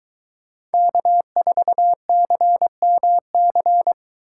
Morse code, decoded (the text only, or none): K4CMC